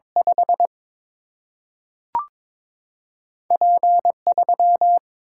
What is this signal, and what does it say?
Morse code, heard as a keyed tone, about 22 wpm, 700 Hz. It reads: 5 P 3
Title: Single-Letter Single-Number 22wpm